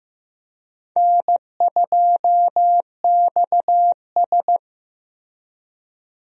Morse code, heard as a keyed tone, about 15 wpm, 700 Hz. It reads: N2XS
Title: USA Call Signs - 4 Characters - Repeated 2x - 15wpm